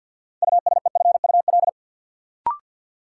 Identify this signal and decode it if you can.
Morse code, heard as a keyed tone, about 52 wpm, 700 Hz. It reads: USEFUL
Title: Sentences from Top 700 Words - New Word Drill - 50wpm